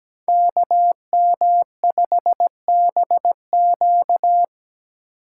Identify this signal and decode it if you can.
Morse code, heard as a keyed tone, about 17 wpm, 700 Hz. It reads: KM5BQ
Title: USA Call Signs - 5 Characters - 2x Character Spacing - 17wpm